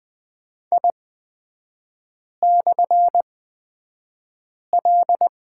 Morse code, heard as a keyed tone, about 20 wpm, spaced as for 5 wpm, 700 Hz. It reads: I/L
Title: 243 Lesson 39e - Letter z - Three Characters 20wpm